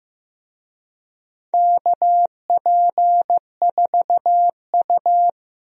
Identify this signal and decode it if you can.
Morse code, heard as a keyed tone, about 15 wpm, 700 Hz. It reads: KP4U
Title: DXCC-100 Call Signs - 4 Characters - 15wpm